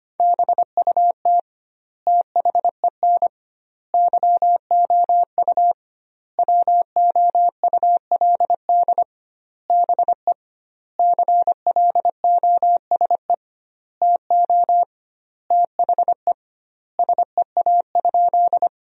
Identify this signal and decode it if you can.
Morse code, heard as a keyed tone, about 25 wpm, 700 Hz. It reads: BUT THEN YOU WOULD BE CLOSE TO THE SEA?
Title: 2x Word-Spacing - Sentences from Top 300 Words - 25wpm